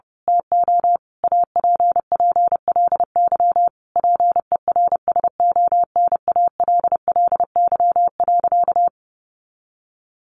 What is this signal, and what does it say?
Morse code, heard as a keyed tone, about 30 wpm, 700 Hz. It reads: TO APPLY PERSONALLY.
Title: Sentences from Top 1400 Words 30wpm